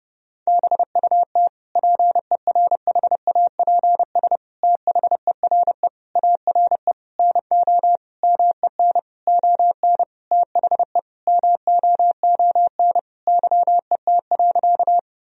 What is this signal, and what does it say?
Morse code, heard as a keyed tone, about 32 wpm, 700 Hz. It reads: BUT PERHAPS THERE ARE NO MEN ON THE MOON YET.
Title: Sentences from Top 1000 Words - Review Part 3 - 30wpm